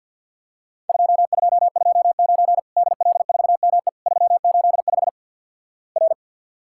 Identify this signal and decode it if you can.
Morse code, heard as a keyed tone, about 50 wpm, 700 Hz. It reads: 2229 DL4ME 385 R
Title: Worked All Europe DX Contest - QTC Exchange - 50wpm